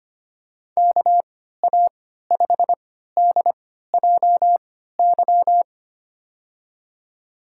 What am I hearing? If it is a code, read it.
Morse code, heard as a keyed tone, about 25 wpm, spaced as for 14 wpm, 700 Hz. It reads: KA5DJY
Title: USA Call Signs - 6 Characters - 3x Character Spacing - 25wpm